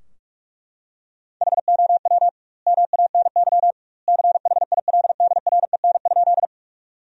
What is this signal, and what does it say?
Morse code, heard as a keyed tone, about 43 wpm, 700 Hz. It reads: HOW MANY CHILDREN?
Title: 2x Word-Spacing - Sentences from Top 300 Words - 45wpm